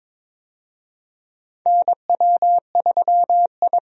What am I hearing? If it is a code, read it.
Morse code, heard as a keyed tone, about 22 wpm, 700 Hz. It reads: NW3I
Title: USA Call Signs - 4 Characters - 22wpm